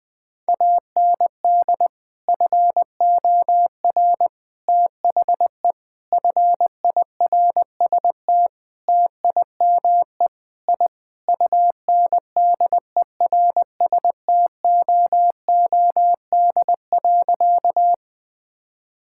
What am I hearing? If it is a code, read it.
Morse code, heard as a keyed tone, about 20 wpm, 700 Hz. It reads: AND FOR THE FIRST TIME I UNDERSTOOD.
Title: Sentences with 5-Word Phrases - Spoken Beginning - 20wpm